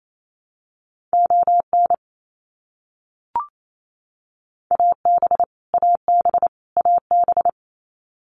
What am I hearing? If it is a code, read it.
Morse code, heard as a keyed tone, about 28 wpm, 700 Hz. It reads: ON AB AB AB
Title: Canadian Provinces and Territories - Repeated 3x - 28wpm